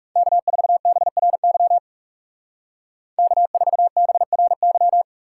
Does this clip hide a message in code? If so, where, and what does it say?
Yes: Morse code from 0.2 to 5.0 s, heard as a keyed tone, about 40 wpm, 700 Hz. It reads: K4BRY K4BRY